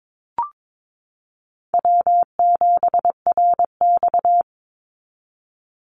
Morse code, heard as a keyed tone, about 22 wpm, 700 Hz. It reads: W7RX
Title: Call Signs Repeated 2x - 22wpm